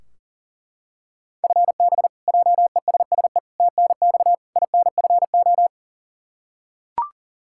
Morse code, heard as a keyed tone, about 37 wpm, 700 Hz. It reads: FB JESSE TNX INFO
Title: K1USN SST QSO Practice 40wpm